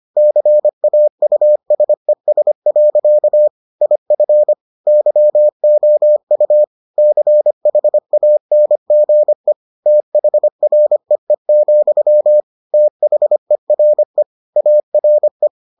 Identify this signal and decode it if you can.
Morse code, heard as a keyed tone, about 25 wpm, 600 Hz. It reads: CAUSES. IF YOU CHANGE THREE, THERE ARE